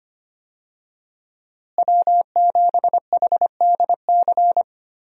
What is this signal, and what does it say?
Morse code, heard as a keyed tone, about 25 wpm, 700 Hz. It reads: W7HDC
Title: USA Call Signs - 5 Characters - Repeated 2x - 25wpm